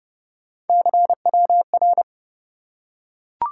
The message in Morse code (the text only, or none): CWR